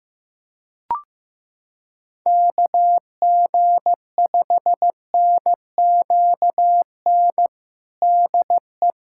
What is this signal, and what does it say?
Morse code, heard as a keyed tone, about 15 wpm, 700 Hz. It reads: KG5NQN DE
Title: USA Callsign DE Callsign 15wpm